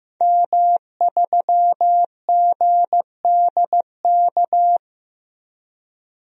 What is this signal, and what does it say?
Morse code, heard as a keyed tone, about 15 wpm, 700 Hz. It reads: M3GDK
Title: DXCC-100 Call Signs - 5 Characters - 15wpm